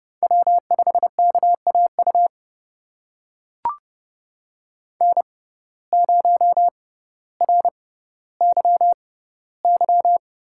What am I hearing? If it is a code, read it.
Morse code, heard as a keyed tone, about 30 wpm, spaced as for 15 wpm, 700 Hz. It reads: W5KAU N0RYY
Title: USA Call Signs - 5 Characters - 6x Character Spacing - 30wpm